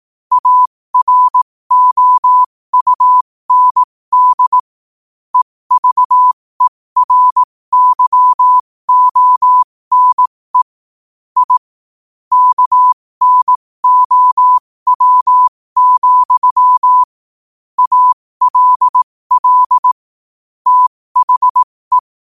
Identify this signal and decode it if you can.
Morse code, heard as a keyed tone, about 18 wpm, spaced as for 15 wpm, 1000 Hz. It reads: AROUND EVERYONE I KNOW, ALL THE